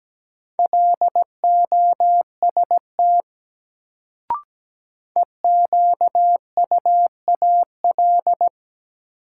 Morse code, heard as a keyed tone, about 17 wpm, 700 Hz. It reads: LOST EQUAL
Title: Sentences from Top 1400 Words - New Word Drill - 17wpm